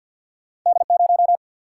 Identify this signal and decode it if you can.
Morse code, heard as a keyed tone, about 54 wpm, 700 Hz. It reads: D0
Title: Single-Letter Single-Number 50wpm